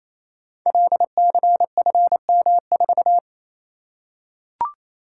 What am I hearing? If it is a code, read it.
Morse code, heard as a keyed tone, about 29 wpm, 700 Hz. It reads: LCFM4
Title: Code Groups - 5 Letter:Numbers - 28wpm